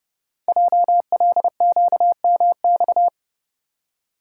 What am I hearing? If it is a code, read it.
Morse code, heard as a keyed tone, about 30 wpm, 700 Hz. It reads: JLQMX
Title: Code Groups - 5 Letters - 30wpm